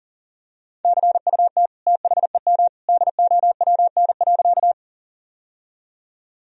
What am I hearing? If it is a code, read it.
Morse code, heard as a keyed tone, about 40 wpm, 700 Hz. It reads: CUT THEM DOWN.
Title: Sentences from Top 1000 Words - Review Part 1 - 40wpm